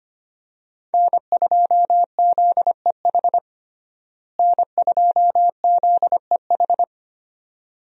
Morse code, heard as a keyed tone, about 25 wpm, 700 Hz. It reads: N2ZEH N2ZEH
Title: USA Call Signs - 5 Characters - Repeated 2x - 25wpm